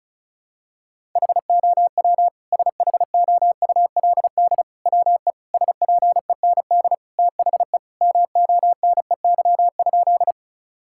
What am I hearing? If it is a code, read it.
Morse code, heard as a keyed tone, about 36 wpm, 700 Hz. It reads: HOW SHOULD WE SPEND THE MONEY?
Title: Sentences from Top 1000 Words - Review Part 5 - 35wpm